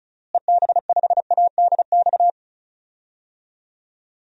Morse code, heard as a keyed tone, about 36 wpm, 700 Hz. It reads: EB5ADX
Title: DXCC-100 Call Signs - 6 Characters - Repeated 2x - 35wpm